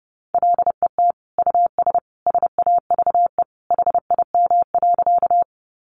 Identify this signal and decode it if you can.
Morse code, heard as a keyed tone, about 30 wpm, 700 Hz. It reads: LET US SAVE HIM.
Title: Sentences from Top 1100 Words 30wpm